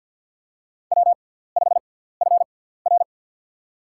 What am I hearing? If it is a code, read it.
Morse code, heard as a keyed tone, about 47 wpm, spaced as for 17 wpm, 700 Hz. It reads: W5FR
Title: USA Call Signs - 6x Character Spacing - 50wpm